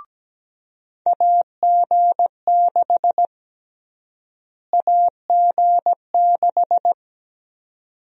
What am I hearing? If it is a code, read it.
Morse code, heard as a keyed tone, about 17 wpm, 700 Hz. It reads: AG6 AG6
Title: USA Call Sign Prefixes - Repeated 2x - 17wpm